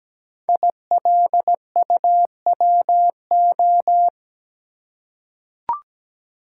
Morse code, heard as a keyed tone, about 17 wpm, 700 Hz. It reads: ILUWO
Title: Code Groups - 5 Letters - 17wpm